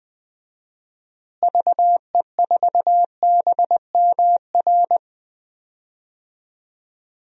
Morse code, heard as a keyed tone, about 21 wpm, 700 Hz. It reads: VE4BMR
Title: DXCC-100 Call Signs - 6 Characters - Repeated 3x - 20wpm